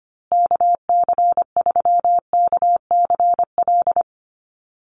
Morse code, heard as a keyed tone, about 25 wpm, 700 Hz. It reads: KC3KCL
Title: USA Call Signs Repeated 3x 25wpm